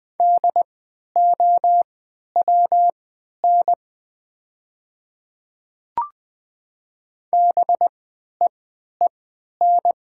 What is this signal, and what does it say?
Morse code, heard as a keyed tone, about 20 wpm, spaced as for 11 wpm, 700 Hz. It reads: DOWN BEEN
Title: 3x Character Spacing - Top 100 Words - 20wpm